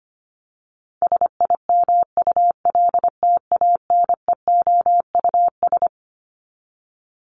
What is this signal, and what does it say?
Morse code, heard as a keyed tone, about 25 wpm, 700 Hz. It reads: SIMULTANEOUS
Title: LY Suffix - Top 100 Words with Base Words 25wpm